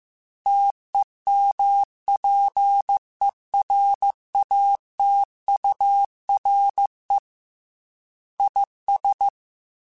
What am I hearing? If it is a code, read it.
Morse code, heard as a keyed tone, about 15 wpm, 800 Hz. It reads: TEMPERATURE IS
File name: Weather Sets / Weather Sets-mp3s / Weather 4 24 @13